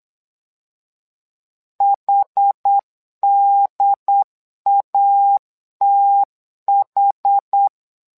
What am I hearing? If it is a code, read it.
Morse code, heard as a keyed tone, about 8 wpm, 800 Hz. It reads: HDATH
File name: LEZIONI / 2 UHDN / 2 UHD 8wpm 2258